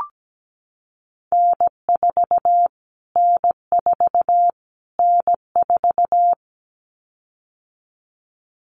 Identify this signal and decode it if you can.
Morse code, heard as a keyed tone, about 17 wpm, 700 Hz. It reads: N4 N4 N4
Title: Callsign Prefixes for Top 25 DXCC - Repeated 3x - 17wpm